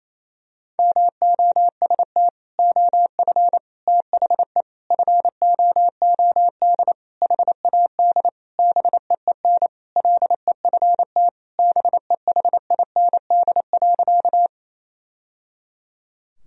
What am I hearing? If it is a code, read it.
Morse code, heard as a keyed tone, about 28 wpm, 700 Hz. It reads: MOST OF THE FOOD HAD BEEN LEFT BEHIND.
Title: Sentences from Top 1000 Words - Review Part 1 - 28wpm